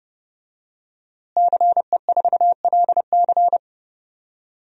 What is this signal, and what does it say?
Morse code, heard as a keyed tone, about 30 wpm, 700 Hz. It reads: CE4LC
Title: DXCC-100 Call Signs - 5 Characters - Repeated 2x - 30wpm